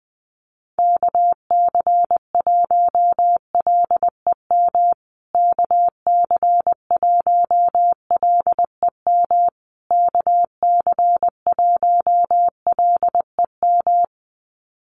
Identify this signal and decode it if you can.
Morse code, heard as a keyed tone, about 20 wpm, 700 Hz. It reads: KC1LEM KC1LEM KC1LEM
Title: USA Call Signs Repeated 3x 20wpm